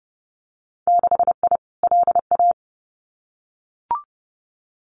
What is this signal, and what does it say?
Morse code, heard as a keyed tone, about 30 wpm, 700 Hz. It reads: 6I LA
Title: Winter Field Day Exchanges Encore 30wpm